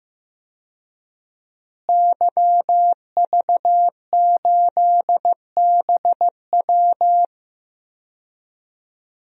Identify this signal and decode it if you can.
Morse code, heard as a keyed tone, about 15 wpm, 700 Hz. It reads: YV8BW
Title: DXCC-100 Call Signs - 5 Characters - Repeated 2x - 15wpm